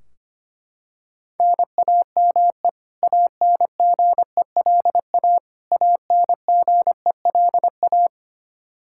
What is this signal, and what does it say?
Morse code, heard as a keyed tone, about 25 wpm, 700 Hz. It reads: NAME ANGELA ANGELA
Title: Practice QSO - Exchange 1 - Beginning - 25wpm